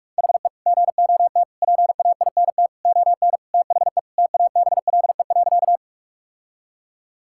Morse code, heard as a keyed tone, about 47 wpm, 700 Hz. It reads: HE GOT PAINT ON THE TABLE.
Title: Sentences from Top 1000 Words - Review Part 6 - 45wpm